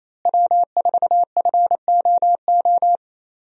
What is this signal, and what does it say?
Morse code, heard as a keyed tone, about 28 wpm, 700 Hz. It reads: W4FOO
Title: USA Call Signs - 5 Characters - Repeated 3x - 28wpm